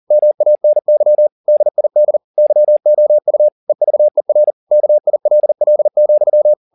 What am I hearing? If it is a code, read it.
Morse code, heard as a keyed tone, about 40 wpm, 600 Hz. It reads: MANY DID YOU EVER KILL,
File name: the old man and the sea 2 40wpm